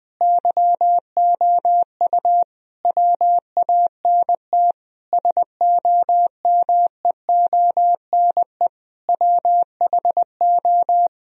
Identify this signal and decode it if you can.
Morse code, heard as a keyed tone, about 20 wpm, 700 Hz. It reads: YOU WANT SOMEONE WHO
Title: Sentences from Top 1000 Words - Review Part 1 - 20wpm